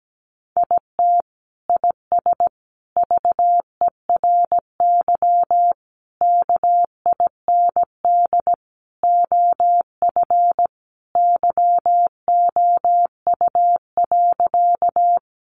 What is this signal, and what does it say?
Morse code, heard as a keyed tone, about 17 wpm, 700 Hz. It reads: IT IS VERY KIND OF YOU.
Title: Sentences from Top 200 Words 17wpm